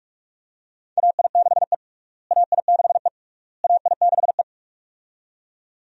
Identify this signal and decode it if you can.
Morse code, heard as a keyed tone, about 46 wpm, 700 Hz. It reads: AI6E AI6E AI6E
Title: USA Call Signs - 4 Characters - Repeated 3x - 45wpm